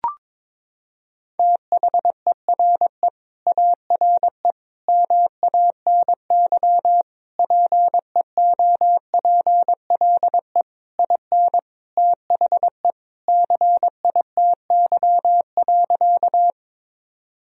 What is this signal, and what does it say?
Morse code, heard as a keyed tone, about 22 wpm, 700 Hz. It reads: THERE ARE MANY PEOPLE IN THE CITY.
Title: Sentences from Top 1000 Words - Review Part 1 - 22wpm